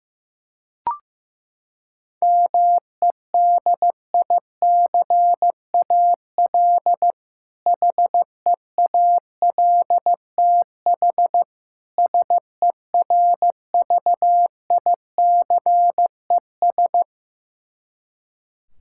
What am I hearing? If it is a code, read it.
Morse code, heard as a keyed tone, about 15 wpm, 700 Hz. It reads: MEDICAL HEALTH SERVICES
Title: K1USN SST QSO Practice 15wpm